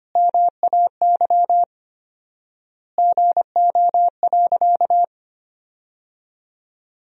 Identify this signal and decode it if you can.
Morse code, heard as a keyed tone, about 25 wpm, 700 Hz. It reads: MAY GO.
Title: 4x Word-Spacing - Sentences from Top 100 Words - 25wpm